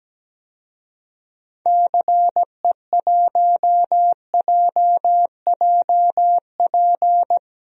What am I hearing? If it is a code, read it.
Morse code, heard as a keyed tone, about 17 wpm, 700 Hz. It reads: CE1JJP
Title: DXCC-100 Call Signs - 6 Characters - Repeated 2x - 17wpm